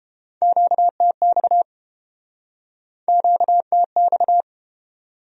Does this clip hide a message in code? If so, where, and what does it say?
Yes: Morse code from 0.4 to 4.4 s, heard as a keyed tone, about 31 wpm, 700 Hz. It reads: QTX QTX